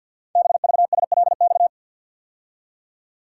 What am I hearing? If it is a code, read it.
Morse code, heard as a keyed tone, about 50 wpm, 700 Hz. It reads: BVSLX
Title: Code Groups - 5 Letter:Numbers - 50wpm